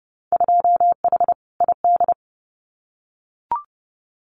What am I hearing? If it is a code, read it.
Morse code, heard as a keyed tone, about 30 wpm, 700 Hz. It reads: 2H ID
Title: Winter Field Day Exchanges 30wpm